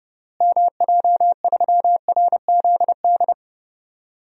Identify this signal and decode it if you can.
Morse code, heard as a keyed tone, about 30 wpm, 700 Hz. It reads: MJ3RZD
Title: DXCC-100 Call Signs - 6 Characters - Repeated 3x - 30wpm